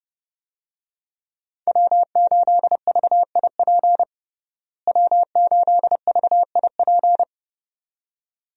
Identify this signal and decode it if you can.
Morse code, heard as a keyed tone, about 32 wpm, 700 Hz. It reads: W8VIP W8VIP
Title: USA Call Signs - 5 Characters - Repeated 2x - 30wpm